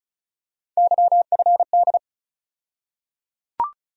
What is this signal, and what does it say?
Morse code, heard as a keyed tone, about 36 wpm, 700 Hz. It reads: YFD
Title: USA Call Sign Postfixes - 3 Letters 35wpm